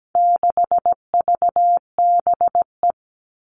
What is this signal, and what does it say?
Morse code, heard as a keyed tone, about 17 wpm, 700 Hz. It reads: 6VBE
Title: All Cut Numbers - 4-Digit 17wpm